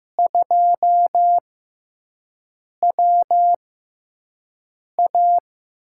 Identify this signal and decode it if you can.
Morse code, heard as a keyed tone, about 15 wpm, spaced as for 5 wpm, 700 Hz. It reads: 2WA
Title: USA Call Signs - 6x Character Spacing - 15wpm